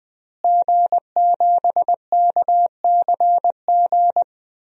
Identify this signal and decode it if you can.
Morse code, heard as a keyed tone, about 20 wpm, 700 Hz. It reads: G7KCG
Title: DXCC-100 Call Signs - 5 Characters - Repeated 3x - 20wpm